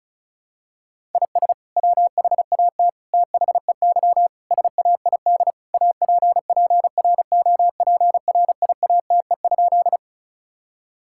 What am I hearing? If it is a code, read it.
Morse code, heard as a keyed tone, about 36 wpm, 700 Hz. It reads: IS WHAT THEY SAID APPROPRIATE?
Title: Sentences from Top 1000 Words - Review Part 5 - 35wpm